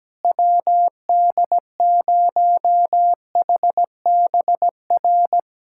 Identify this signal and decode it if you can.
Morse code, heard as a keyed tone, about 17 wpm, 700 Hz. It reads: WD0HBR
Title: USA Callsign DE Callsign - 6 Characters - 17wpm